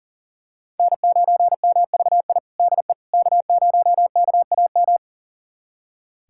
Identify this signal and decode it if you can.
Morse code, heard as a keyed tone, about 40 wpm, 700 Hz. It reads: N9MVI DE K0KAM
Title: USA Callsign DE Callsign - 5 Characters - 40wpm